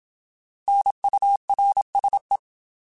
Morse code, heard as a keyed tone, about 27 wpm, 800 Hz. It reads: NURSE
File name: Alphabetical / Alphabetical mp3s / Ns @25